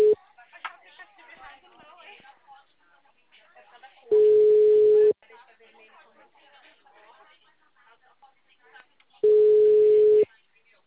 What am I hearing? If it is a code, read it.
Ringback tone (Europe, 425 Hz), with speech.